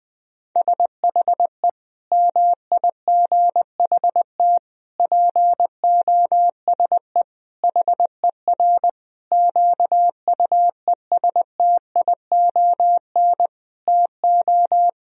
Morse code, heard as a keyed tone, about 20 wpm, 700 Hz. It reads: SHE MIGHT POSE HER QUESTION TO
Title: Sentences from Top 1000 Words - Review Part 2 - 20wpm